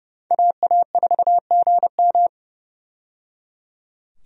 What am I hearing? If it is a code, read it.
Morse code, heard as a keyed tone, about 30 wpm, 700 Hz. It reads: AA4GM
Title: USA Call Signs - 5 Characters - 30wpm